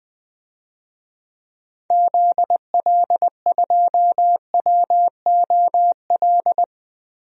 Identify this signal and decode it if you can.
Morse code, heard as a keyed tone, about 20 wpm, 700 Hz. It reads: ZL2WOL
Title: DXCC-100 Call Signs - 6 Characters - 20wpm